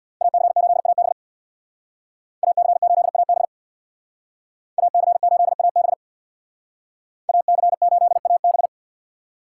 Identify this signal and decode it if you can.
Morse code, heard as a keyed tone, about 64 wpm, 700 Hz. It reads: AC8AB AC8AB AC8AB AC8AB